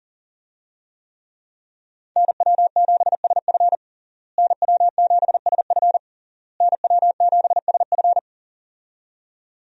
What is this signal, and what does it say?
Morse code, heard as a keyed tone, about 40 wpm, 700 Hz. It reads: NW7SF NW7SF NW7SF
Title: USA Call Signs - 5 Characters - Repeated 3x - 40wpm